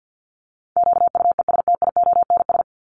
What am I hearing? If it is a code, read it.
Morse code, heard as a keyed tone, about 50 wpm, 700 Hz. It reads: QUESTIONS
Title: ES:S Suffix - Top 100 Words with Base Words 50wpm